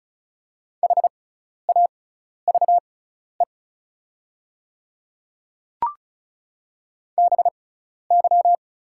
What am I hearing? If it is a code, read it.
Morse code, heard as a keyed tone, about 36 wpm, spaced as for 12 wpm, 700 Hz. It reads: HAVE BY
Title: 6x Character Spacing - Top 100 Words - 35wpm